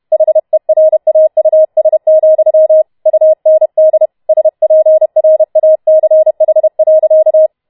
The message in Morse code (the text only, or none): HERAUS, UND SPRACH.